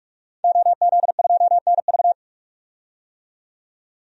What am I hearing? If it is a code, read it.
Morse code, heard as a keyed tone, about 46 wpm, 700 Hz. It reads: OZ2NV
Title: DXCC-100 Call Signs - 5 Characters - 45wpm